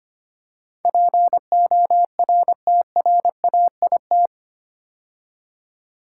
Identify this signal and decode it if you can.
Morse code, heard as a keyed tone, about 25 wpm, 690 Hz. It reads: PORTRAIT
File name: 8 Letter Words - First 3 Letters Given - 25wpm